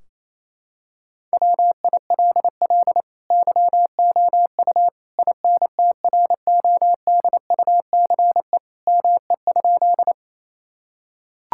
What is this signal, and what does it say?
Morse code, heard as a keyed tone, about 28 wpm, 700 Hz. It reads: WILL YOU INTRODUCE ME?
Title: Sentences from Top 1600 Words 28wpm